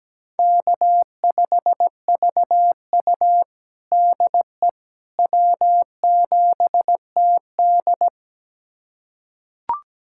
Morse code, heard as a keyed tone, about 17 wpm, 700 Hz. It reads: K5VU DE W7TD
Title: USA Callsign DE Callsign - 4 Characters - 17wpm